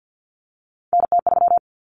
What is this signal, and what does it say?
Morse code, heard as a keyed tone, about 50 wpm, 700 Hz. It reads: NT3
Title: Common Cut Numbers - 3-Digit 50wpm